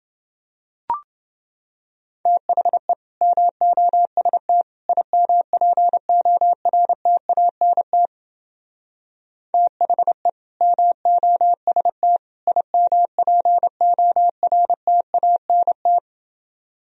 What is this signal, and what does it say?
Morse code, heard as a keyed tone, about 29 wpm, 700 Hz. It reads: THE MOST IMPORTANT THE MOST IMPORTANT